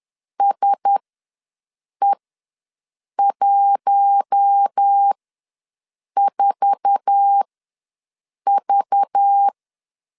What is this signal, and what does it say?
Morse code, heard as a keyed tone, about 10 wpm, spaced as for 6 wpm, 800 Hz. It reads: SE14V
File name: LEZIONI / 9 2867 ws 10wpm 4215